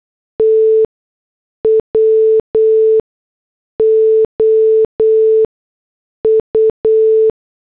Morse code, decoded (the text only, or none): TWO U